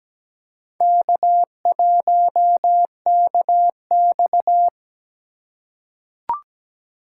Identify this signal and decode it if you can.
Morse code, heard as a keyed tone, about 17 wpm, 700 Hz. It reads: K1KX
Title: USA Call Signs - 4 Characters - 2x Character Spacing - 17wpm